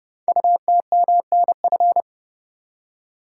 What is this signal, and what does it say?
Morse code, heard as a keyed tone, about 30 wpm, 700 Hz. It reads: UTMNF
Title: Code Groups - 5 Letters - 30wpm